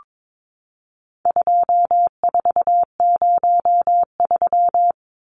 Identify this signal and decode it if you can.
Morse code, heard as a keyed tone, about 22 wpm, 700 Hz. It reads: 2403
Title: Numbers - 4-Digit 22wpm